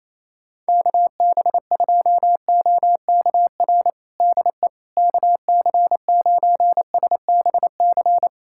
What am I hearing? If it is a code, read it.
Morse code, heard as a keyed tone, about 28 wpm, 700 Hz. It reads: KB2OKR DE KC9SBC
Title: USA Callsign DE Callsign - 6 Characters - 28wpm